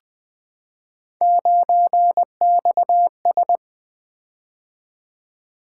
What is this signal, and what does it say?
Morse code, heard as a keyed tone, about 20 wpm, 700 Hz. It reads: 9XS
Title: Three Letter-Number Combinations - 20wpm